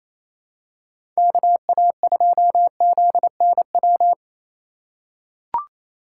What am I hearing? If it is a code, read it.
Morse code, heard as a keyed tone, about 28 wpm, 700 Hz. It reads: KA2ZNW
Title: USA Call Signs - 6 Characters - 3x Character Spacing - 28wpm